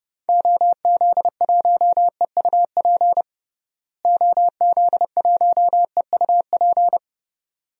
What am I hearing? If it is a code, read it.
Morse code, heard as a keyed tone, about 31 wpm, 700 Hz. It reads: OZ1EUP OZ1EUP
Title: DXCC-100 Call Signs - 6 Characters - Repeated 2x - 30wpm